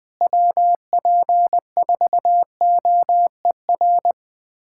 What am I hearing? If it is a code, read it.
Morse code, heard as a keyed tone, about 20 wpm, 700 Hz. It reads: WP4OER
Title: USA Callsign DE Callsign - 6 Characters - 20wpm